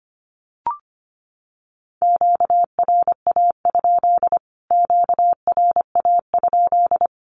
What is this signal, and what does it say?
Morse code, heard as a keyed tone, about 25 wpm, 700 Hz. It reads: QRA? QRA?
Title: Q-Codes Repeated 3x 25wpm